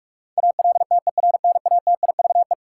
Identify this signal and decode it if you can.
Morse code, heard as a keyed tone, about 46 wpm, 700 Hz. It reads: ALTERNATIVE
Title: Sentences from Top 1000 Words - Review Part 4 - New Word Drill - 45wpm